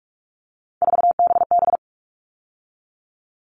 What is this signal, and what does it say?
Morse code, heard as a keyed tone, about 44 wpm, 700 Hz. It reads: 4BB
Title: All Cut Numbers - 3-Digit 45wpm